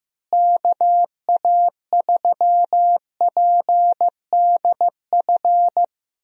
Morse code, heard as a keyed tone, about 15 wpm, 700 Hz. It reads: KA3PDF
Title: USA Call Signs - 6 Characters - Repeated 3x - 15wpm